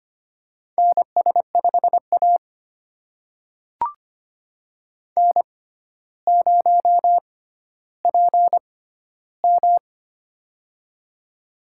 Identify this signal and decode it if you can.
Morse code, heard as a keyed tone, about 25 wpm, spaced as for 13 wpm, 700 Hz. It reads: NS5A N0PM
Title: USA Call Signs - 4 Characters - 6x Character Spacing - 25wpm